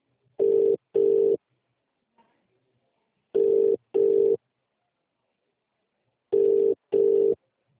Ringback tone (UK), 400 + 450 Hz.